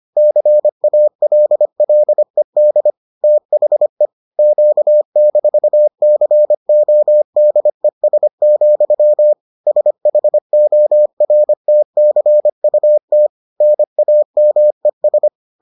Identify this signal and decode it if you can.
Morse code, heard as a keyed tone, about 25 wpm, 600 Hz. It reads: CALLED THE Q-CODES, SHORTCUT NAMES